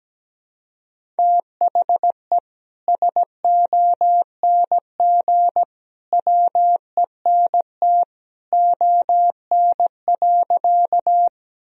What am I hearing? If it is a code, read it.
Morse code, heard as a keyed tone, about 17 wpm, 700 Hz. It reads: THE SONG WENT ON.
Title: Sentences from Top 1000 Words - Review Part 2 - 17wpm